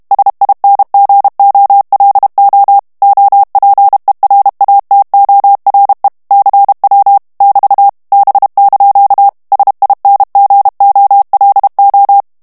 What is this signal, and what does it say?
Morse code, heard as a keyed tone, about 32 wpm, 800 Hz. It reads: SINGOLO OPERATORE CW <BT> B) SINGOLO
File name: texte clair 2 vitesse 30 mots minute